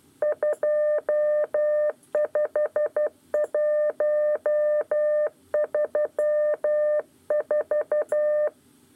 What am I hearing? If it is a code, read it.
Morse code, heard as a keyed tone, about 12 wpm, 600 Hz. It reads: 25134
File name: 8 lezione cw 1